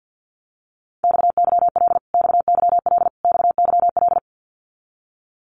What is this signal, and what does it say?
Morse code, heard as a keyed tone, about 50 wpm, 700 Hz. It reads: XYL XYL XYL
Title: Abbreviations Repeated 3x - 50wpm